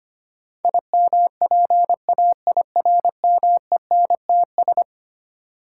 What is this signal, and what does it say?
Morse code, heard as a keyed tone, about 25 wpm, 700 Hz. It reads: IMPAIRMENTS
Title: Sets of 5 Words Encore - New Word Drill - 25wpm